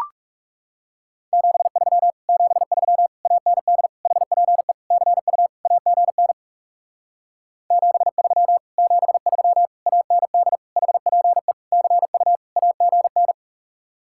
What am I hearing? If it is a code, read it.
Morse code, heard as a keyed tone, about 39 wpm, 700 Hz. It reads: 73 73 AND HPE CU AGN 73 73 AND HPE CU AGN